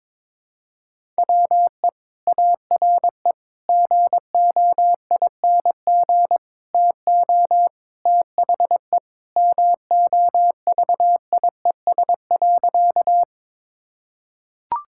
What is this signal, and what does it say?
Morse code, heard as a keyed tone, about 22 wpm, 700 Hz. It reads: WE ARE GOING TO THE MOVIES.
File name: practice / 22wpm / Sentences with 4-Word Phrases - 22wpm